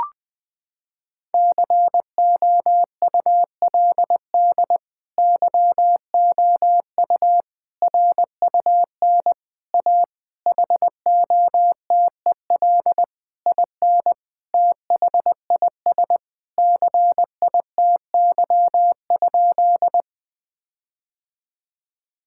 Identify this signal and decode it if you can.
Morse code, heard as a keyed tone, about 20 wpm, 700 Hz. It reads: COULD YOU RUN A HOTEL IN THIS CITY?
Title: Sentences from Top 1600 Words 20wpm